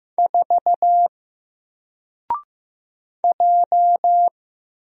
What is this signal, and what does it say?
Morse code, heard as a keyed tone, about 15 wpm, 700 Hz. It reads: 4 J
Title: Single Letter-Number 15wpm